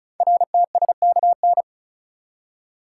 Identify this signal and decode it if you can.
Morse code, heard as a keyed tone, about 36 wpm, 700 Hz. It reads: RTSKN
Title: Code Groups - 5 Letters - 35wpm